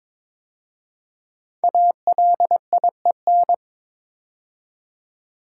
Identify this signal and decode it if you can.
Morse code, heard as a keyed tone, about 22 wpm, 700 Hz. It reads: ALIEN
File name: 5 Letter Words - First 3 Letters Given - 22wpm